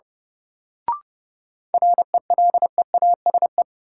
Morse code, heard as a keyed tone, about 30 wpm, 700 Hz. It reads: RELEASE
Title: Sentences from Top 1100 Words - New Word Drill - 30wpm